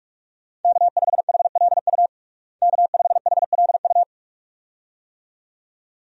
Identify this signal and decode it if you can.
Morse code, heard as a keyed tone, about 46 wpm, 700 Hz. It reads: K5HLU K5HLU
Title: USA Call Signs - 5 Characters - Repeated 2x - 45wpm